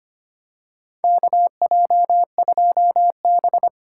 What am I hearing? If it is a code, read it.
Morse code, heard as a keyed tone, about 25 wpm, 700 Hz. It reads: KJ2B
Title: USA Call Signs - 4 Characters - 3x Character Spacing - 25wpm